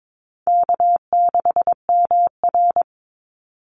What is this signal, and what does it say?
Morse code, heard as a keyed tone, about 22 wpm, 700 Hz. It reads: K6MR
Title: Easy Call Signs - 22wpm